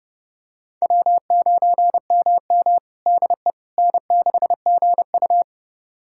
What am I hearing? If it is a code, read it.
Morse code, heard as a keyed tone, about 32 wpm, 700 Hz. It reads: W9MM DE N6GU
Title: USA Callsign DE Callsign - 4 Characters - 30wpm